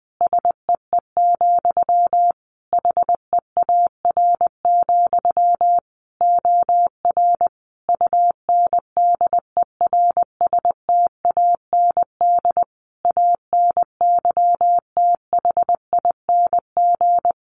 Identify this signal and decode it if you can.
Morse code, heard as a keyed tone, about 20 wpm, 700 Hz. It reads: SEE, HEAR, OR UNDERSTAND ANYTHING
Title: Sentences from Top 1300 Words 20wpm